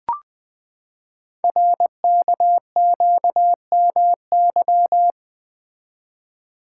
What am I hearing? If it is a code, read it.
Morse code, heard as a keyed tone, about 20 wpm, 700 Hz. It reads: RKQMY
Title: Code Groups - 5 Letter:Numbers - 20wpm